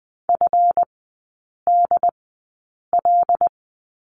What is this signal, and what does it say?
Morse code, heard as a keyed tone, about 20 wpm, spaced as for 8 wpm, 690 Hz. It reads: FDL
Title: 205 Lesson 33e - Letter j - Three Characters 20wpm